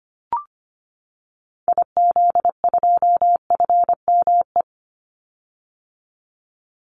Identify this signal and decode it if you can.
Morse code, heard as a keyed tone, about 25 wpm, 700 Hz. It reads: IZ2FME
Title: Call Signs - 25wpm